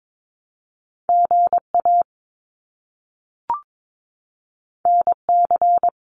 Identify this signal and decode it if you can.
Morse code, heard as a keyed tone, about 22 wpm, 700 Hz. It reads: GA NC
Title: US States Repeated 3x 22wpm